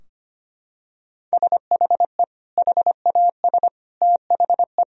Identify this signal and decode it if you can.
Morse code, heard as a keyed tone, about 25 wpm, 700 Hz. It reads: SHE HAS THE
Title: Sentences from Top 1000 Words - Review Part 3 - 25wpm